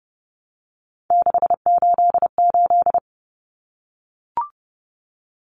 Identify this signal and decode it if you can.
Morse code, heard as a keyed tone, about 30 wpm, 700 Hz. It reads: 688
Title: Numbers - 3-Digit 30wpm